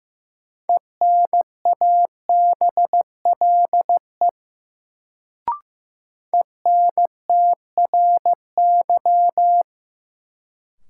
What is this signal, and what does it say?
Morse code, heard as a keyed tone, about 15 wpm, 700 Hz. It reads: ENABLE ENTRY
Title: Sentences from Top 1500 Words - New Word Drill - 15wpm